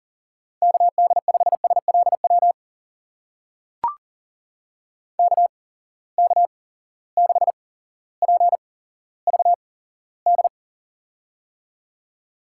Morse code, heard as a keyed tone, about 40 wpm, spaced as for 17 wpm, 700 Hz. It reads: KD5SLW KK6PVD
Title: USA Call Signs - 6 Characters - 8x Character Spacing - 40wpm